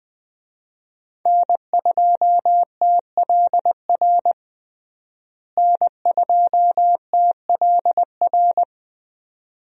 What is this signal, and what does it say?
Morse code, heard as a keyed tone, about 20 wpm, 700 Hz. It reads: N2TLR N2TLR
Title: USA Call Signs - 5 Characters - Repeated 2x - 20wpm